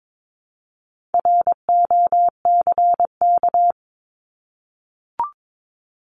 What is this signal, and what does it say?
Morse code, heard as a keyed tone, about 22 wpm, 700 Hz. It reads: ROCK
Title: Top 500 Words 22wpm